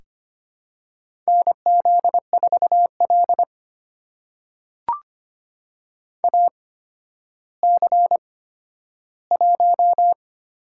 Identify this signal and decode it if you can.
Morse code, heard as a keyed tone, about 25 wpm, spaced as for 12 wpm, 700 Hz. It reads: NZ4L AC1
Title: USA Call Signs - 8x Character Spacing - 25wpm